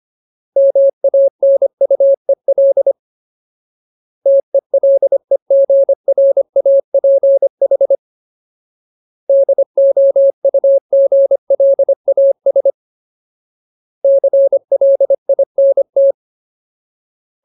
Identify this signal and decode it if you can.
Morse code, heard as a keyed tone, about 25 wpm, 550 Hz. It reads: MANUEL TELEGRAPH DOUGLAS CLINT
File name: Ultimate Word List / 25WPM Ultimate